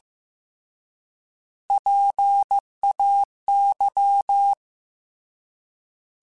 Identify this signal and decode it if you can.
Morse code, heard as a keyed tone, about 15 wpm, 800 Hz. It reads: PAY
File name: Alphabetical / Alphabetical mp3s / Ps @13